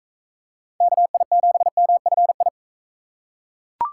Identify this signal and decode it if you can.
Morse code, heard as a keyed tone, about 41 wpm, 700 Hz. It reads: KI7MFI